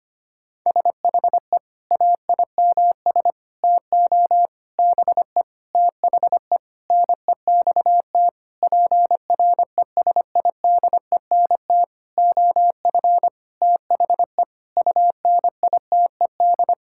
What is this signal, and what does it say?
Morse code, heard as a keyed tone, about 25 wpm, 700 Hz. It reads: SHE AIMS TO BE THE NEXT PRESIDENT OF THE UNITED
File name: practice / 25wpm / Sentences with 9-Word Phrases - 25wpm